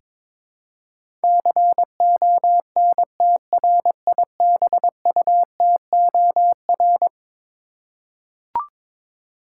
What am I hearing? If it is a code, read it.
Morse code, heard as a keyed tone, about 22 wpm, 700 Hz. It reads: CONTRIBUTOR
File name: practice / 22wpm / Eleven Letter Words - 22wpm